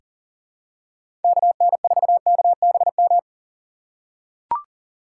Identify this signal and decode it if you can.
Morse code, heard as a keyed tone, about 37 wpm, 700 Hz. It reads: KN4KBM
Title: USA Call Signs - 6 Characters - 2x Character Spacing - 40wpm